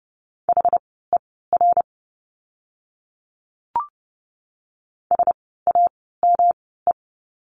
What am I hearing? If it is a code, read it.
Morse code, heard as a keyed tone, about 30 wpm, spaced as for 17 wpm, 700 Hz. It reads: HER SAME
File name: course / 30:10wpm / 230 Lesson 37f - Letter x - Words 30wpm